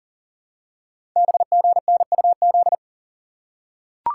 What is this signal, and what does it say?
Morse code, heard as a keyed tone, about 44 wpm, 700 Hz. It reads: BGNUZ
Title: Code Groups - 5 Letters - 40wpm